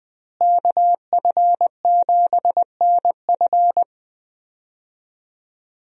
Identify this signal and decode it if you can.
Morse code, heard as a keyed tone, about 20 wpm, 700 Hz. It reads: KF7NF
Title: USA Call Signs - 5 Characters - Repeated 3x - 20wpm